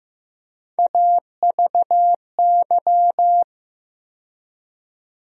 Morse code, heard as a keyed tone, about 15 wpm, 700 Hz. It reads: AVY
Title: USA Call Sign Postfixes - 3 Letters 15wpm